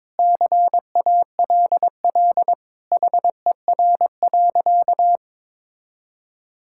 Morse code, heard as a keyed tone, about 22 wpm, 700 Hz. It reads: CALL HER.
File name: practice / 22wpm / Sentences with 4-Word Phrases - 22wpm